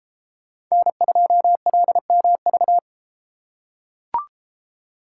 Morse code, heard as a keyed tone, about 33 wpm, 700 Hz. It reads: N2LMV